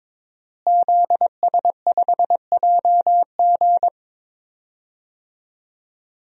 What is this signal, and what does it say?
Morse code, heard as a keyed tone, about 22 wpm, 700 Hz. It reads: ZS5JG
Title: DXCC-100 Call Signs - 5 Characters - Repeated 3x - 22wpm